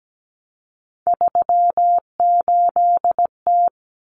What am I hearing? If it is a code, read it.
Morse code, heard as a keyed tone, about 17 wpm, 700 Hz. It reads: 38T
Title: Common Cut Numbers - 3-Digit 17wpm